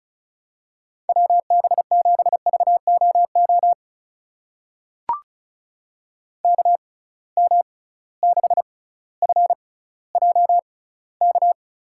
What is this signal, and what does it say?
Morse code, heard as a keyed tone, about 36 wpm, spaced as for 18 wpm, 700 Hz. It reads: WB7VOO KM6FJK
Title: USA Call Signs - 6x Character Spacing - 35wpm